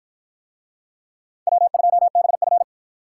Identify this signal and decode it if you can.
Morse code, heard as a keyed tone, about 50 wpm, 700 Hz. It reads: W2BF